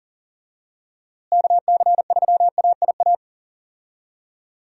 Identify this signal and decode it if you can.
Morse code, heard as a keyed tone, about 40 wpm, 700 Hz. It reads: KC3AIA
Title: USA Call Signs - 6 Characters - Repeated 3x - 40wpm